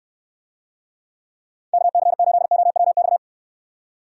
Morse code, heard as a keyed tone, about 69 wpm, 700 Hz. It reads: KC8OWX